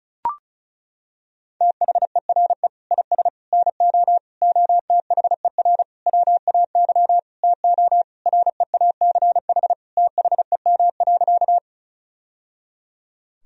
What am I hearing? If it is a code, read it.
Morse code, heard as a keyed tone, about 34 wpm, 700 Hz. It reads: THERE IS NO OTHER WAY TO REACH THEM.
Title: Sentences from Top 1000 Words - Review Part 2 - 35wpm